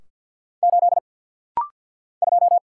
Morse code, heard as a keyed tone, about 50 wpm, 700 Hz. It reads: 8 2
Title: Single Letter-Number Alt - Rapid-Fire 50wpm